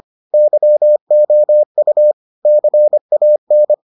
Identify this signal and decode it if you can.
Morse code, heard as a keyed tone, about 25 wpm, 600 Hz. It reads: YOU CAN